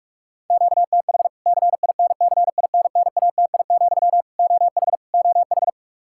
Morse code, heard as a keyed tone, about 46 wpm, 700 Hz. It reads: QTH CINCINNATI, OH OH
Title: K1USN SST QSO Practice 45wpm